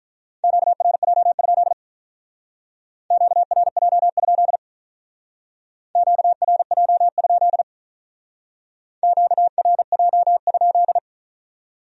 Morse code, heard as a keyed tone, about 46 wpm, 700 Hz. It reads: QRJ? QRJ? QRJ? QRJ?